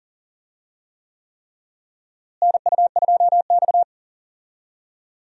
Morse code, heard as a keyed tone, about 40 wpm, 700 Hz. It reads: NU2X
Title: USA Call Signs - 4 Characters - Repeated 3x - 40wpm